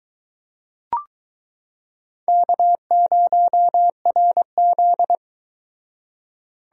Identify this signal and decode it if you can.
Morse code, heard as a keyed tone, about 22 wpm, 700 Hz. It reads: K0RZ